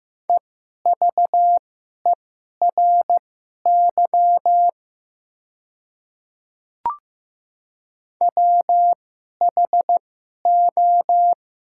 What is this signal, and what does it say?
Morse code, heard as a keyed tone, about 16 wpm, spaced as for 11 wpm, 700 Hz. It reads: EVERY WHO
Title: 2x Character Spacing - Top 200 Words - 15wpm